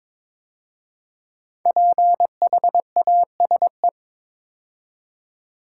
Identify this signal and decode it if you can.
Morse code, heard as a keyed tone, about 22 wpm, 700 Hz. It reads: PHASE
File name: 5 Letter Words - First 3 Letters Given - 22wpm